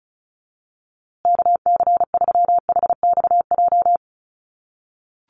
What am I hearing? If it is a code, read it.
Morse code, heard as a keyed tone, about 34 wpm, 700 Hz. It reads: KC3HXJ
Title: USA Call Signs Repeated 3x 35wpm